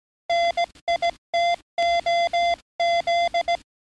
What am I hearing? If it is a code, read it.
Morse code, heard as a keyed tone, about 17 wpm, 690 Hz. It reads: NITOZ
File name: training / NAVPERS 11049to11060 / NAVPERS 11057 RA RB